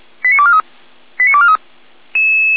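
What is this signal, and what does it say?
Five-tone selective call (ZVEI), heard as quick runs of five tones, tones of about 70 ms. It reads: 89243 89243
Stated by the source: ZVEI five-tone selective call received via SDR.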